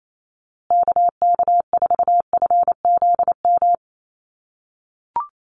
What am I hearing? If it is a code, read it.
Morse code, heard as a keyed tone, about 28 wpm, 700 Hz. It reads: KK4FZM
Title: USA Call Signs Repeated 2x 28wpm